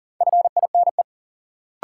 Morse code, heard as a keyed tone, about 40 wpm, 700 Hz. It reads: FINE
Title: Sentences from Top 500 Words - New Word Drill - 40wpm